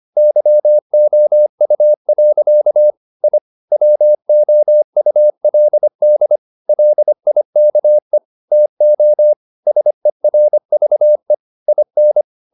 Morse code, heard as a keyed tone, about 25 wpm, 600 Hz. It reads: YOU. I WOULD LIKE TO SERVE IN
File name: the old man and the sea part 1 25wpm